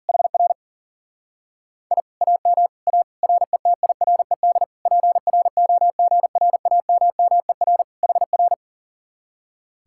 Morse code, heard as a keyed tone, about 50 wpm, 700 Hz. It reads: HR I AM A RETIRED PROGRAMMER HR
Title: Speed-Racing Common QSO Phrases - 40wpm